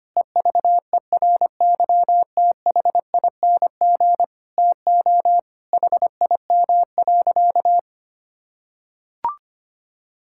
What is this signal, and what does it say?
Morse code, heard as a keyed tone, about 25 wpm, 700 Hz. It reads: EVERYTHING TO HIM.
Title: Sentences from Top 1600 Words 25wpm